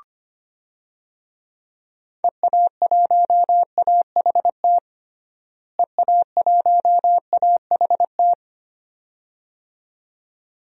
Morse code, heard as a keyed tone, about 25 wpm, 700 Hz. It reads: EA1AHT EA1AHT
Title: DXCC-100 Call Signs - 6 Characters - Repeated 2x - 25wpm